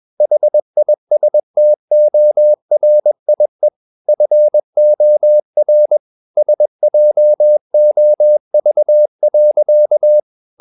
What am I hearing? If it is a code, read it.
Morse code, heard as a keyed tone, about 21 wpm, 600 Hz. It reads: HISTORIE FOR SJOV.
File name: Ferie QRS 21wpm